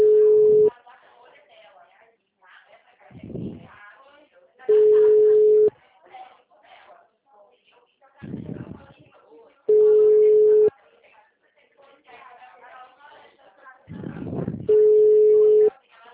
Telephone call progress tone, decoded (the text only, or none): ringback tone